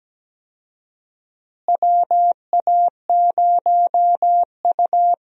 Morse code, heard as a keyed tone, about 17 wpm, 700 Hz. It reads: WA0U
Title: USA Call Signs - 4 Characters - Repeated 3x - 17wpm